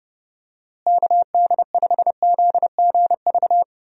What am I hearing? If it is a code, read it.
Morse code, heard as a keyed tone, about 30 wpm, 700 Hz. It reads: KD5ZGV
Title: USA Call Signs - 6 Characters - Repeated 2x - 30wpm